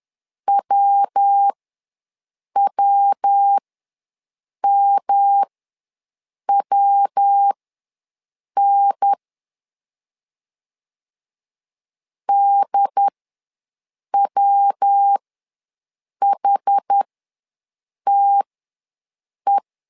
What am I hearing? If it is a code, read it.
Morse code, heard as a keyed tone, about 10 wpm, spaced as for 6 wpm, 800 Hz. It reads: WWMWN DWHTE
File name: LEZIONI / 3 MWRO / 3 MW ws 10wpm 2213